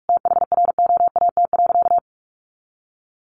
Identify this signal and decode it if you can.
Morse code, heard as a keyed tone, about 44 wpm, 700 Hz. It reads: THROAT.
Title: A Sound Of Thunder 45wpm